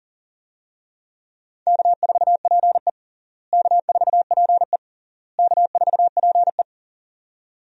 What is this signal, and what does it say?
Morse code, heard as a keyed tone, about 40 wpm, 700 Hz. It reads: K4PE K4PE K4PE
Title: USA Call Signs - 4 Characters - Repeated 3x - 40wpm